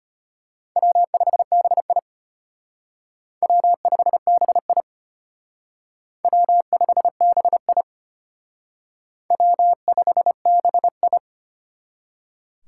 Morse code, heard as a keyed tone, about 33 wpm, 700 Hz. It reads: W5BI W5BI W5BI W5BI